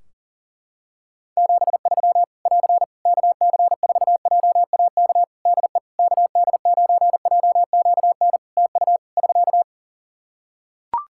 Morse code, heard as a keyed tone, about 37 wpm, 700 Hz. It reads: 73 <AR> KC4JAK DE KD9JQN TU <SK>
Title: Practice QSO - Exchange 4 - Ending - 40wpm